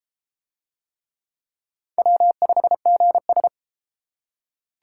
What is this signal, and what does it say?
Morse code, heard as a keyed tone, about 31 wpm, 700 Hz. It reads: W5GS